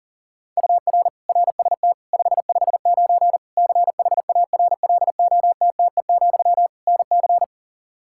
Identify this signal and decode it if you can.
Morse code, heard as a keyed tone, about 40 wpm, 700 Hz. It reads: UR RST 559 CHARLOTTE, NC
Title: Practice QSO - Exchange 1 - Beginning - 40wpm